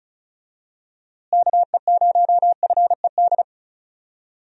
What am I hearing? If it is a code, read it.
Morse code, heard as a keyed tone, about 36 wpm, 700 Hz. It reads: KE0FED
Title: USA Call Signs - 6 Characters - Repeated 3x - 35wpm